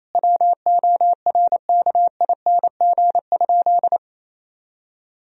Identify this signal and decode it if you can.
Morse code, heard as a keyed tone, about 28 wpm, 700 Hz. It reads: WORKING?
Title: Sentences from Top 1000 Words - Review Part 4 - 28wpm